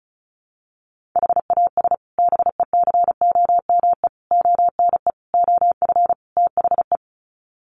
Morse code, heard as a keyed tone, about 34 wpm, 700 Hz. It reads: HAS BECOME ONE OF THE
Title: Sets of 5 Words Encore 35wpm